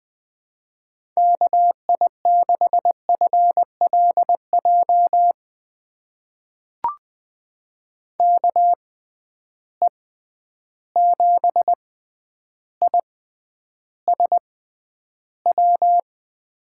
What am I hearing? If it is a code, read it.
Morse code, heard as a keyed tone, about 20 wpm, spaced as for 10 wpm, 700 Hz. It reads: KI6FLJ KE7ISW
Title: USA Call Signs - 6x Character Spacing - 20wpm